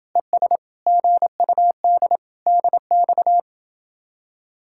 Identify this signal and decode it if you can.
Morse code, heard as a keyed tone, about 31 wpm, 700 Hz. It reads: ES GUD DX